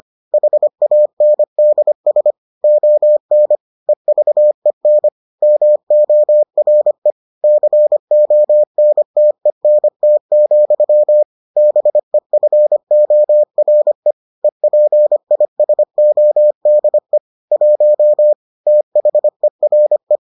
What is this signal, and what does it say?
Morse code, heard as a keyed tone, about 25 wpm, 600 Hz. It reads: HANDS ON EVEN MORE CONTENT, BEFORE EPISODE 1 THERE